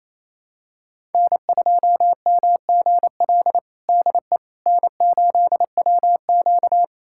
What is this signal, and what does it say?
Morse code, heard as a keyed tone, about 28 wpm, 700 Hz. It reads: N2MGL DE N8WQ
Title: USA Callsign DE Callsign 28wpm